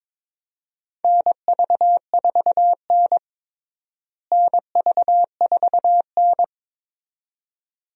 Morse code, heard as a keyed tone, about 22 wpm, 700 Hz. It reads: NV4N NV4N
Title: USA Call Signs - 4 Characters - Repeated 2x - 22wpm